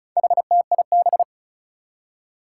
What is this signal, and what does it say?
Morse code, heard as a keyed tone, about 36 wpm, 700 Hz. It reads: HTIB
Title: Four Challenging Letter-Number Combinations 35wpm